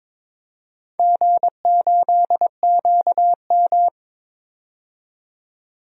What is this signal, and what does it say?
Morse code, heard as a keyed tone, about 23 wpm, 700 Hz. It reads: G8QM
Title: DXCC-100 Call Signs - 4 Characters - 22wpm